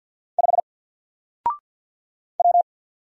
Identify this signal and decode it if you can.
Morse code, heard as a keyed tone, about 48 wpm, 700 Hz. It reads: 5 W
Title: Single Letter-Number 50wpm